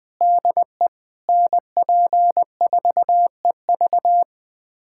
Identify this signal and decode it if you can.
Morse code, heard as a keyed tone, about 20 wpm, 700 Hz. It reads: DE NP4EV
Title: USA Callsign DE Callsign 20wpm